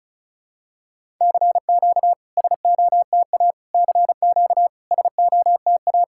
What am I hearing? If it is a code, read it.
Morse code, heard as a keyed tone, about 36 wpm, 690 Hz. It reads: CQ SOTA CQ SOTA
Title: SOTA QSO Practice 35wpm